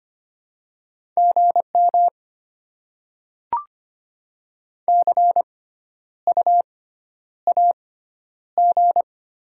Morse code, heard as a keyed tone, about 25 wpm, spaced as for 11 wpm, 700 Hz. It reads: GM CU AG
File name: Top 20 QSO Elements - 6x Character Spacing - 25wpm